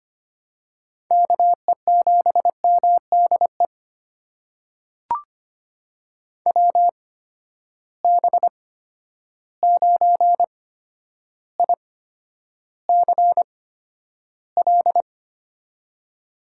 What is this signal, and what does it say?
Morse code, heard as a keyed tone, about 25 wpm, spaced as for 11 wpm, 700 Hz. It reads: KE7MDE WB9ICL
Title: USA Call Signs - 6 Characters - 8x Character Spacing - 25wpm